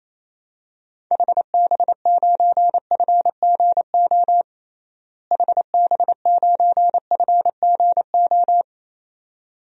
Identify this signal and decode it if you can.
Morse code, heard as a keyed tone, about 27 wpm, 700 Hz. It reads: HB9FGO HB9FGO
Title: DXCC-100 Call Signs - 6 Characters - Repeated 2x - 28wpm